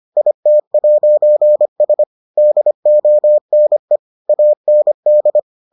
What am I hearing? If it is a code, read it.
Morse code, heard as a keyed tone, about 25 wpm, 600 Hz. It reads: IT'S DONE AND